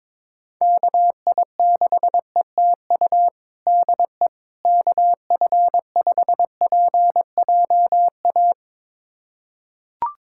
Morse code, heard as a keyed tone, about 22 wpm, 700 Hz. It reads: KI6ETU DE KF5PJA
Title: USA Callsign DE Callsign - 6 Characters - 22wpm